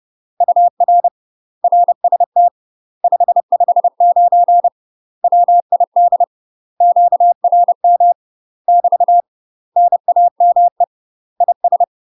Morse code, heard as a keyed tone, about 30 wpm, 700 Hz. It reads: UR RST 559 WID QRM <BT> NAME IS
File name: UK MASTER QSOs TRAINING / 30WPM / UK MASTER QSOs 0032